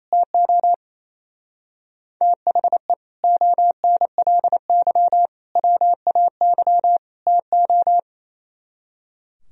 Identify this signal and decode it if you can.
Morse code, heard as a keyed tone, about 34 wpm, 690 Hz. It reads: TO THE ONLY WAY TO
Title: Speed-Racing - Sets of 4 Words 28wpm